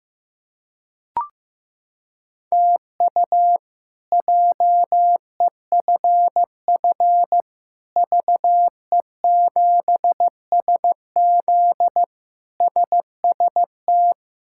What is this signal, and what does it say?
Morse code, heard as a keyed tone, about 15 wpm, 700 Hz. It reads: TU JEFF VE7SZ SST
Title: K1USN SST QSO Practice - 15wpm